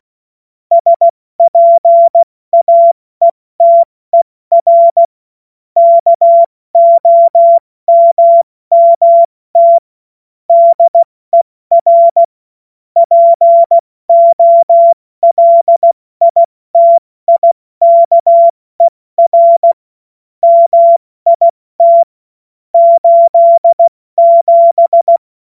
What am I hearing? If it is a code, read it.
Morse code, heard as a keyed tone, about 16 wpm, spaced as for 14 wpm, 680 Hz. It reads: SPAETER KOMMT DER POLITIKER MIT 87
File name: Witz-23-September-2025-70-F80